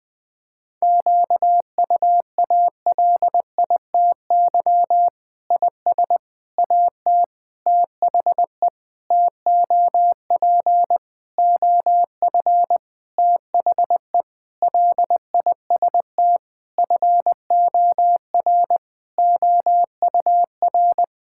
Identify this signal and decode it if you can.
Morse code, heard as a keyed tone, about 20 wpm, 700 Hz. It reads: QUALITY IS AT THE TOP OF THE LIST FOR OUR
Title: Sentences with 6-Word Phrases - 20wpm